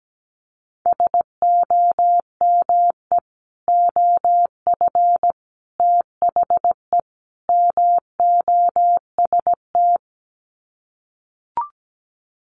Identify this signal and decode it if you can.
Morse code, heard as a keyed tone, about 17 wpm, 700 Hz. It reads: SOME OF THE MOST
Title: Sets of 4 Words 17wpm